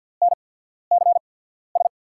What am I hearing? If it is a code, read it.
Morse code, heard as a keyed tone, about 50 wpm, spaced as for 14 wpm, 700 Hz. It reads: NCS
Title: USA Call Signs - 5 Characters - 8x Character Spacing - 50wpm